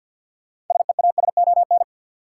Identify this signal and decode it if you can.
Morse code, heard as a keyed tone, about 53 wpm, 700 Hz. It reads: SEASON
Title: Sets of 5 Words - New Word Drill - 50wpm